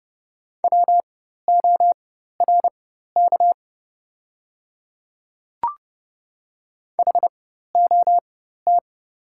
Morse code, heard as a keyed tone, about 29 wpm, spaced as for 14 wpm, 700 Hz. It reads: WORK HOT
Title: 4x Character Spacing - Top 200 Words - 30wpm